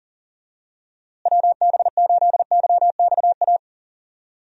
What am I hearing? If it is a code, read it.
Morse code, heard as a keyed tone, about 40 wpm, 700 Hz. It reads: WB8YXA
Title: USA Call Signs - 6 Characters - Repeated 2x - 40wpm